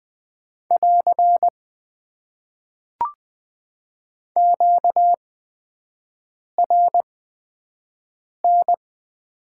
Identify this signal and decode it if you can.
Morse code, heard as a keyed tone, about 20 wpm, spaced as for 5 wpm, 700 Hz. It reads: <AR> QRN
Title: Top 90 QSO Elements - 8x Character Spacing - 20wpm